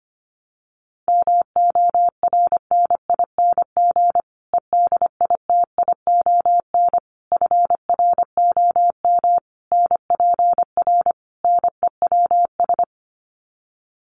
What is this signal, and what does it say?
Morse code, heard as a keyed tone, about 25 wpm, 700 Hz. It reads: MORNING EDITION FROM NPR NEWS
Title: Sets of 5 Words Encore 25wpm